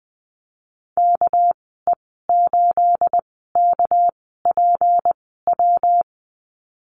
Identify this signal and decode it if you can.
Morse code, heard as a keyed tone, about 20 wpm, spaced as for 15 wpm, 700 Hz. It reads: KE8KPW
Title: 219 Lesson 35g - Number 0 - Call Signs 20wpm